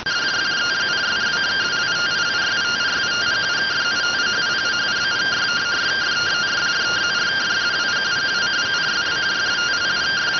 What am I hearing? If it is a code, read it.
Two-tone teleprinter signal, about 170 Hz shift.